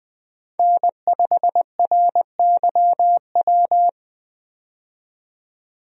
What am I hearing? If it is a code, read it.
Morse code, heard as a keyed tone, about 20 wpm, 700 Hz. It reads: N5RYW
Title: USA Call Signs - 5 Characters - 20wpm